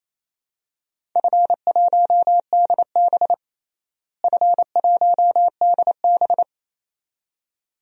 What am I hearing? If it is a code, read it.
Morse code, heard as a keyed tone, about 27 wpm, 700 Hz. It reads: F1DB F1DB
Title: DXCC-100 Call Signs - 4 Characters - Repeated 2x - 28wpm